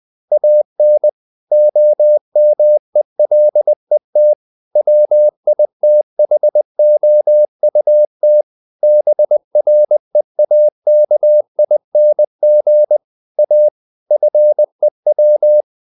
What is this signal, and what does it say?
Morse code, heard as a keyed tone, about 20 wpm, 600 Hz. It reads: AN OMELET WITHOUT BREAKING A FEW
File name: CW / Practice / EnglishSayings20